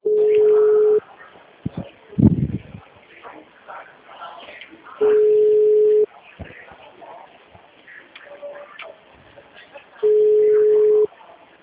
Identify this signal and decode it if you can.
Ringback tone (Europe, 425 Hz).